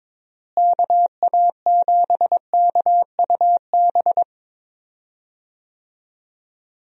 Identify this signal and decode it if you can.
Morse code, heard as a keyed tone, about 22 wpm, 700 Hz. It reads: KA7KUB
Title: USA Call Signs - 6 Characters - Repeated 2x - 22wpm